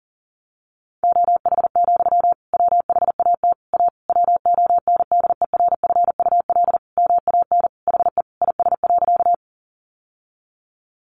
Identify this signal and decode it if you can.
Morse code, heard as a keyed tone, about 40 wpm, 700 Hz. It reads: OH, WHAT A WONDERFUL MAN HE IS.
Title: Sentences from Top 1100 Words 40wpm